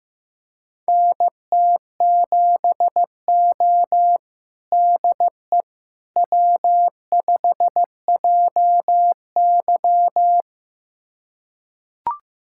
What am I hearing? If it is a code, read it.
Morse code, heard as a keyed tone, about 15 wpm, 700 Hz. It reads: NT7O DE W5JY
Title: USA Callsign DE Callsign - 4 Characters - 15wpm